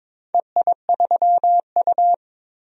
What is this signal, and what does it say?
Morse code, heard as a keyed tone, about 21 wpm, 700 Hz. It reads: EI3U
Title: DXCC-100 Call Signs - 4 Characters - 22wpm